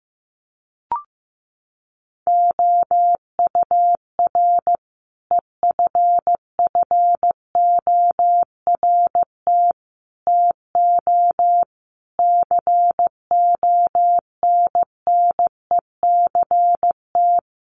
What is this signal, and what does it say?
Morse code, heard as a keyed tone, about 15 wpm, 700 Hz. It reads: OUR EFFORT TO CONNECT
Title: Sentences from Top 1500 Words 15wpm